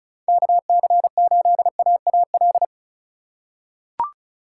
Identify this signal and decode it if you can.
Morse code, heard as a keyed tone, about 34 wpm, 700 Hz. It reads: KC8AAL
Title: USA Call Signs - 6 Characters - 3x Character Spacing - 35wpm